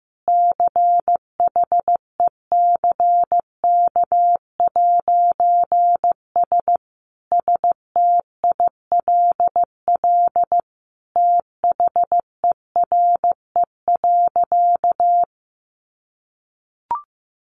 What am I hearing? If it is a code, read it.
Morse code, heard as a keyed tone, about 15 wpm, 700 Hz. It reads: CHECK'S STILL THERE.
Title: A Sound Of Thunder 15wpm